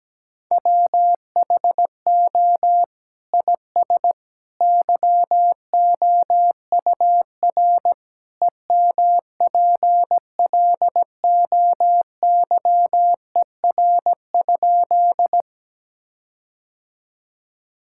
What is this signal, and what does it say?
Morse code, heard as a keyed tone, about 17 wpm, 700 Hz. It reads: WHO IS YOUR EMPLOYER?
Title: Sentences from Top 1600 Words 17wpm